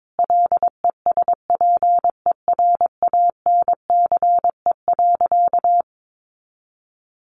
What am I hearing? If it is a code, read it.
Morse code, heard as a keyed tone, about 22 wpm, 700 Hz. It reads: LESPERANCE.
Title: A Sound Of Thunder 22wpm